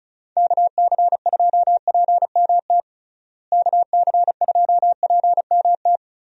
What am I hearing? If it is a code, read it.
Morse code, heard as a keyed tone, about 36 wpm, 700 Hz. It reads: KC2PMT KC2PMT
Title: USA Call Signs - 6 Characters - Repeated 2x - 35wpm